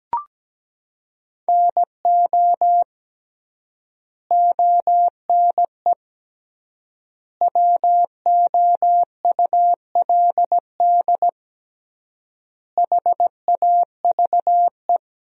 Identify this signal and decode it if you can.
Morse code, heard as a keyed tone, about 17 wpm, 700 Hz. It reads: NO ONE WOULD HAVE
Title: 3x Word-Spacing - Sentences from Top 300 Words - 17wpm